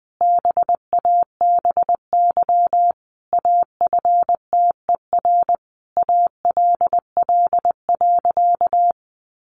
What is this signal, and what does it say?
Morse code, heard as a keyed tone, about 20 wpm, 700 Hz. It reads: BABY AFTER ALL.
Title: Sentences from Top 1300 Words 20wpm